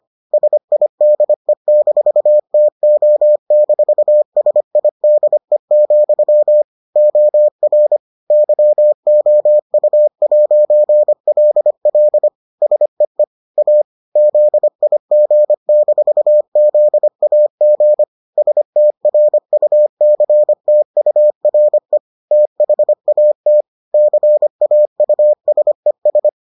Morse code, heard as a keyed tone, about 25 wpm, 600 Hz. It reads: SIDE-TO-SIDE, OR YOU'LL SEE A ZIG-ZAG STRUCTURE THAT CAUSES